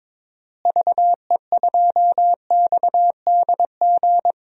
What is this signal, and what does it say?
Morse code, heard as a keyed tone, about 22 wpm, 700 Hz. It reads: VE2XDG
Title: DXCC-100 Call Signs - 6 Characters - Repeated 3x - 22wpm